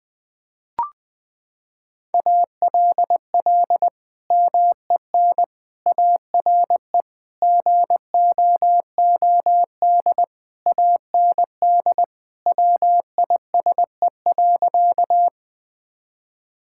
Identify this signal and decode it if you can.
Morse code, heard as a keyed tone, about 20 wpm, 700 Hz. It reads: ALL MEN ARE GOOD AND WISE.
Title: Sentences from Top 1600 Words 20wpm